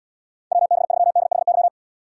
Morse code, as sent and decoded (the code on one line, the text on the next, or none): .-- -... ...-- -.. .... -.--
WB3DHY